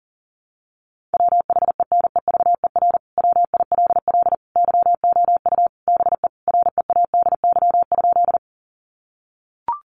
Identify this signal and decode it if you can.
Morse code, heard as a keyed tone, about 37 wpm, 700 Hz. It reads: WHENEVER WILL YOU BE READY?
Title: Sentences from Top 1200 Words 40wpm